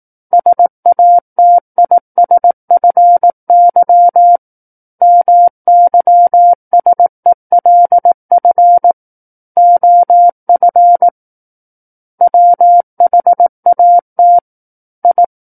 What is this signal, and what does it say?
Morse code, heard as a keyed tone, about 18 wpm, 700 Hz. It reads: SATISFY MYSELF OF WHAT I
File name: 18 WPM